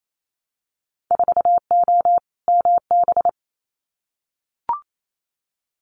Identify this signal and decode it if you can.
Morse code, heard as a keyed tone, about 28 wpm, 700 Hz. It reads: 4O MB
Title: Winter Field Day Exchanges Encore 28wpm